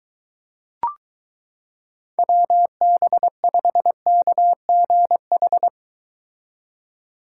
Morse code, heard as a keyed tone, about 22 wpm, 700 Hz. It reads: WB5KGH